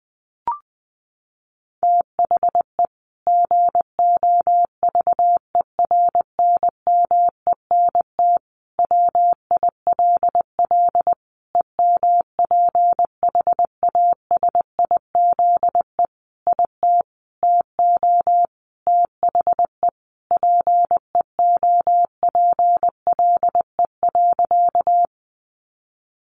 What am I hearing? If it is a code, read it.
Morse code, heard as a keyed tone, about 20 wpm, 700 Hz. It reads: THE GOVERNMENT WILL EMPHASIZE IT TO THE PEOPLE.
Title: Sentences from Top 1500 Words 20wpm